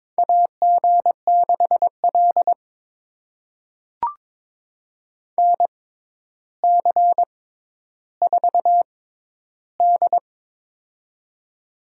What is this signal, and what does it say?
Morse code, heard as a keyed tone, about 22 wpm, spaced as for 11 wpm, 700 Hz. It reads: AG6L NC4D
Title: USA Call Signs - 4 Characters - 6x Character Spacing - 22wpm